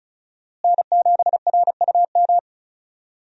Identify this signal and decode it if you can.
Morse code, heard as a keyed tone, about 35 wpm, 700 Hz. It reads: N7RUM
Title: USA Call Signs - 5 Characters - Repeated 2x - 35wpm